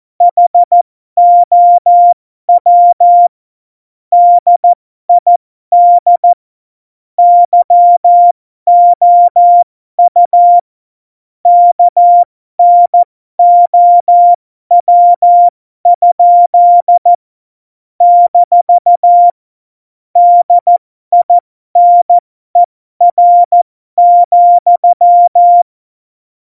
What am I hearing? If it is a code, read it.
Morse code, heard as a keyed tone, about 14 wpm, spaced as for 12 wpm, 700 Hz. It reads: HOW DID YOU KNOW? - DINER,
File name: Morsekurs / Witze